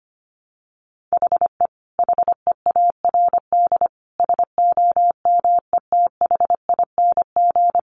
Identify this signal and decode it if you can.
Morse code, heard as a keyed tone, about 25 wpm, 700 Hz. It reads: HE HEARD SOMETHING
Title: Sentences from Top 800 Words 25wpm